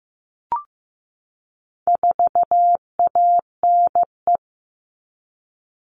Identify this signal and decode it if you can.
Morse code, heard as a keyed tone, about 15 wpm, 700 Hz. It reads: 4ANE
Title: All Cut Numbers - 4-Digit 15wpm